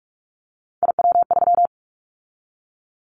Morse code, heard as a keyed tone, about 46 wpm, 700 Hz. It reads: IW3
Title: Callsign Prefixes for Top 50 DXCC - 45wpm